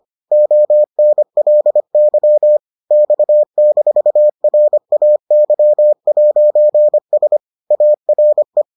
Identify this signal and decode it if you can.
Morse code, heard as a keyed tone, about 25 wpm, 600 Hz. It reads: ONLY X-RAY'S ARE